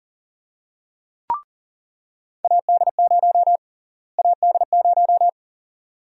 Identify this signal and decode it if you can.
Morse code, heard as a keyed tone, about 40 wpm, 700 Hz. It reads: AD0 AD0
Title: USA Call Sign Prefixes - Repeated 2x - 40wpm